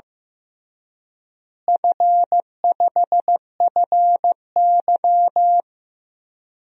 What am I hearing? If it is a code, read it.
Morse code, heard as a keyed tone, about 15 wpm, 700 Hz. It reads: F5FY
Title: DXCC-100 Call Signs - 4 Characters - Repeated 3x - 15wpm